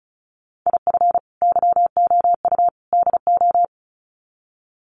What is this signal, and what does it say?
Morse code, heard as a keyed tone, about 34 wpm, 700 Hz. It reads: IF YOU DO
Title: Sets of 3 Words 35wpm